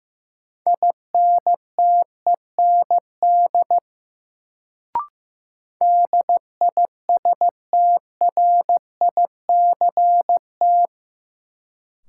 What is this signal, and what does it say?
Morse code, heard as a keyed tone, about 15 wpm, 700 Hz. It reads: INTEND DISTRICT
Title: Sentences from Top 1600 Words - New Word Drill - 15wpm